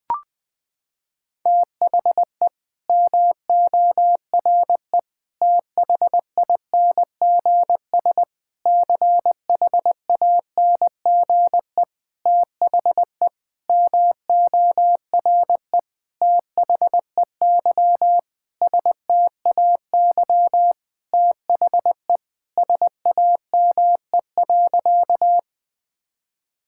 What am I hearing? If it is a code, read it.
Morse code, heard as a keyed tone, about 20 wpm, 700 Hz. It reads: THE MORE THINGS CHANGE THE MORE THEY STAY THE SAME.
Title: Sentences with 8-Word Phrases - 20wpm